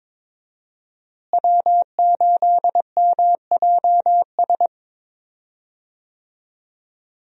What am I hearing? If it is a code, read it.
Morse code, heard as a keyed tone, about 22 wpm, 700 Hz. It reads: W8MJS
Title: USA Call Signs - 5 Characters - Repeated 2x - 22wpm